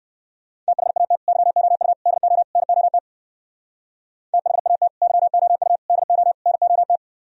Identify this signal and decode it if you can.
Morse code, heard as a keyed tone, about 58 wpm, 700 Hz. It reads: THAT YOU DO NOT THAT YOU DO NOT